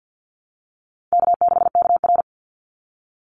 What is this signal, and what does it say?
Morse code, heard as a keyed tone, about 50 wpm, 700 Hz. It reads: K6KR
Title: Easy Call Signs - 50wpm